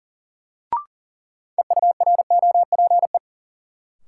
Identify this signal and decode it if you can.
Morse code, heard as a keyed tone, about 40 wpm, 700 Hz. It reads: EUROPE
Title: Sets of 5 Words Encore - New Word Drill - 40wpm